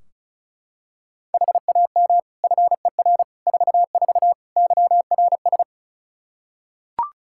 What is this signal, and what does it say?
Morse code, heard as a keyed tone, about 34 wpm, 700 Hz. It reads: HAM FER 44 YRS
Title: Practice QSO - Exchange 3 - More Info - 35wpm